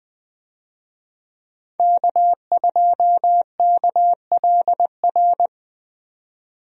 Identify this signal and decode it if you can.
Morse code, heard as a keyed tone, about 20 wpm, 700 Hz. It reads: K2KLR
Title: USA Call Signs - 5 Characters - 20wpm